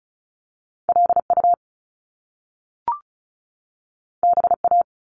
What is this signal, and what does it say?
Morse code, heard as a keyed tone, about 36 wpm, 700 Hz. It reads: LU BA
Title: Top Two Letter Combinations in Words 35wpm